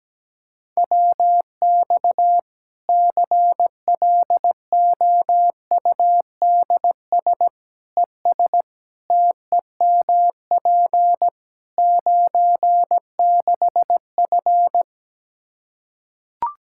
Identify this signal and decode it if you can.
Morse code, heard as a keyed tone, about 17 wpm, 700 Hz. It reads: WX CLOUDS ES TEMP 96F
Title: Practice QSO - Exchange 2 - Info - 17wpm